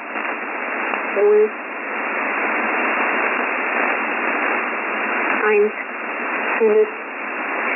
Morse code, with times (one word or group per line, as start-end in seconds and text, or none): none